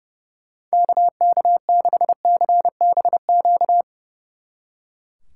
Morse code, heard as a keyed tone, about 39 wpm, 700 Hz. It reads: KK6CBQ